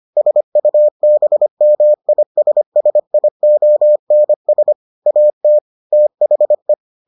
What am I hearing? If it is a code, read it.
Morse code, heard as a keyed tone, about 25 wpm, 600 Hz. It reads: SUBMISSIONS AT THE